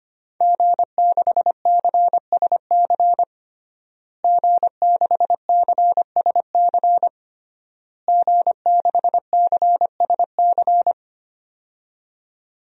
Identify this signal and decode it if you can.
Morse code, heard as a keyed tone, about 25 wpm, 700 Hz. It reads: G6CSC G6CSC G6CSC
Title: DXCC-100 Call Signs - 5 Characters - Repeated 3x - 25wpm